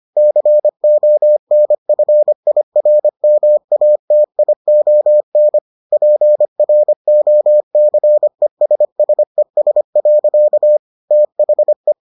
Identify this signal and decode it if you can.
Morse code, heard as a keyed tone, about 25 wpm, 600 Hz. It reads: CONFIRMATION PROCESSES. THE